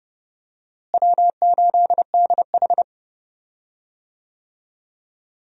Morse code, heard as a keyed tone, about 30 wpm, 700 Hz. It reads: W8DH
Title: USA Call Signs - 4 Characters - Repeated 2x - 30wpm